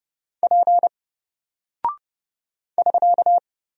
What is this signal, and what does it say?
Morse code, heard as a keyed tone, about 29 wpm, 700 Hz. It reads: P <SK>
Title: Single Character (Letter:Number:Punctuation:Prosign) Alt 30wpm